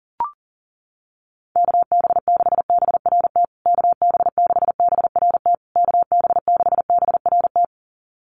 Morse code, heard as a keyed tone, about 40 wpm, 700 Hz. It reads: KB6BRT KB6BRT KB6BRT
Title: USA Call Signs Repeated 3x 40wpm